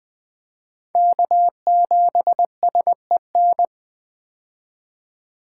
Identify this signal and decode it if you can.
Morse code, heard as a keyed tone, about 20 wpm, 700 Hz. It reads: K7SEN
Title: 2023 POTA Hunter Call Signs - 20wpm